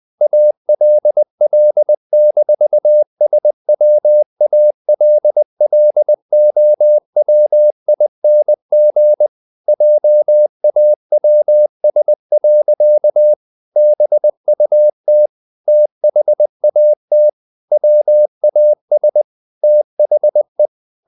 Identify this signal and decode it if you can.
Morse code, heard as a keyed tone, about 20 wpm, 600 Hz. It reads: ALL-SWALLOWING JAWS. BUT THAT WAS THE
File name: the old man and the sea part 2 20wpm